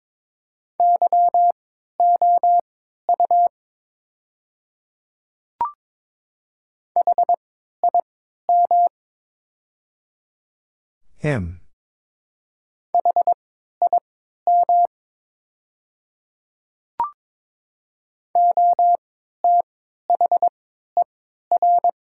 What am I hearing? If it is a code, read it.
Morse code, heard as a keyed tone, about 22 wpm, spaced as for 12 wpm, 700 Hz. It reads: YOU HIM HIM OTHER
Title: 3x Character Spacing - Top 100 Words - 22wpm